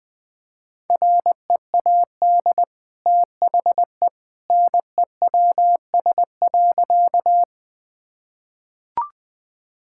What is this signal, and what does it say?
Morse code, heard as a keyed tone, about 21 wpm, 700 Hz. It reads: READ THE NEWS.
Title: Easy Sentences with Hints - 20wpm